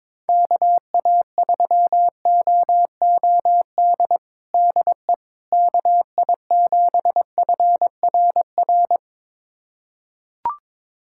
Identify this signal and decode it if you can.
Morse code, heard as a keyed tone, about 22 wpm, 700 Hz. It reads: KA3OOD DE KI7FRR
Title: USA Callsign DE Callsign 22wpm